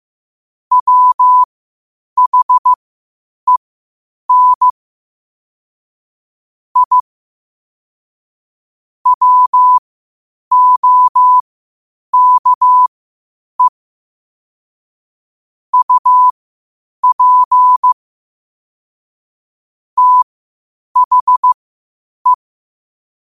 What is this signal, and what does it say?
Morse code, heard as a keyed tone, about 15 wpm, spaced as for 8 wpm, 1000 Hz. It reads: WHEN I WOKE UP THE
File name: qotdfiles / 8wpm